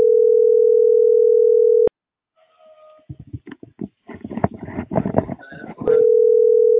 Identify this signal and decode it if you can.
Ringback tone (North America), 440 + 480 Hz.